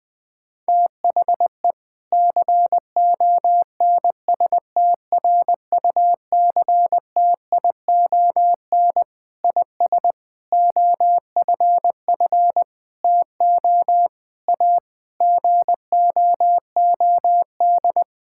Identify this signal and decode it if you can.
Morse code, heard as a keyed tone, about 20 wpm, 700 Hz. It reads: THE CONSTRUCTION IS OFF TO A GOOD
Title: Sentences from Top 1600 Words 20wpm